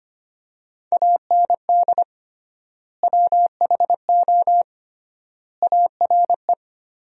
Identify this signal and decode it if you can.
Morse code, heard as a keyed tone, about 25 wpm, 700 Hz. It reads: AND WHO ARE
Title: 3x Word-Spacing - Sentences from Top 100 Words - 25wpm